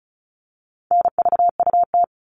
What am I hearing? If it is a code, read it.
Morse code, heard as a keyed tone, about 34 wpm, 700 Hz. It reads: NVUT
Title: All Cut Numbers - 4-Digit 35wpm